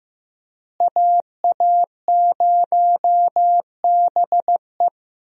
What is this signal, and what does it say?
Morse code, heard as a keyed tone, about 15 wpm, 700 Hz. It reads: AA0BE
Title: USA Call Signs - 5 Characters - Repeated 3x - 15wpm